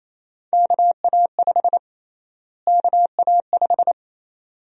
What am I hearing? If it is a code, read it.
Morse code, heard as a keyed tone, about 28 wpm, 700 Hz. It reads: KA5 KA5
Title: USA Call Sign Prefixes - Repeated 2x - 28wpm